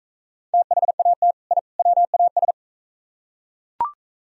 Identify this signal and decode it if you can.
Morse code, heard as a keyed tone, about 41 wpm, 700 Hz. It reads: THAT I WAS